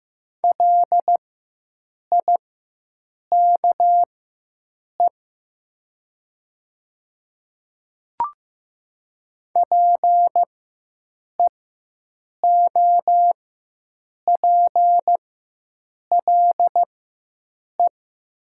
Morse code, heard as a keyed tone, about 15 wpm, spaced as for 7 wpm, 700 Hz. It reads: LIKE PEOPLE
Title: 4x Character Spacing - Top 100 Words - 15wpm